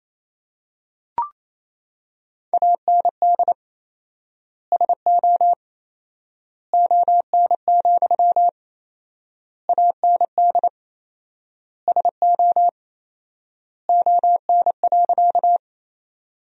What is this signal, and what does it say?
Morse code, heard as a keyed tone, about 28 wpm, 700 Hz. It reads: AND SO ON, AND SO ON.
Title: 4x Word-Spacing - Sentences from Top 100 Words - 28wpm